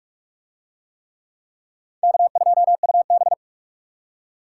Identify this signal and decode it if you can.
Morse code, heard as a keyed tone, about 41 wpm, 700 Hz. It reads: K2UB